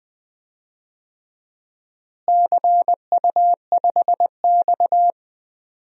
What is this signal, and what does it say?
Morse code, heard as a keyed tone, about 20 wpm, 700 Hz. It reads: CU5X
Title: DXCC-100 Call Signs - 4 Characters - Repeated 2x - 20wpm